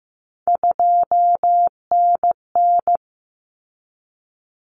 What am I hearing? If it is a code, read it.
Morse code, heard as a keyed tone, about 15 wpm, 700 Hz. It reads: 2NN
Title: Common Cut Numbers - 3-Digit 15wpm